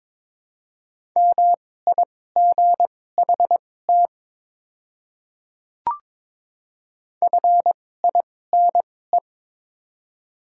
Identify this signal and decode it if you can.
Morse code, heard as a keyed tone, about 23 wpm, spaced as for 16 wpm, 700 Hz. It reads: MIGHT FINE
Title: 2x Character Spacing - Top 500 Words - 22wpm